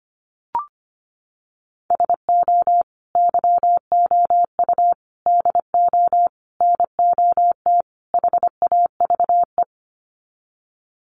Morse code, heard as a keyed tone, about 25 wpm, 700 Hz. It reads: SO YOU DO NOT HAVE
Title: Sets of 5 Words Encore 25wpm